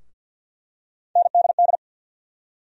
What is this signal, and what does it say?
Morse code, heard as a keyed tone, about 48 wpm, 700 Hz. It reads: NDD
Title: USA Call Sign Postfixes - 3 Letters 50wpm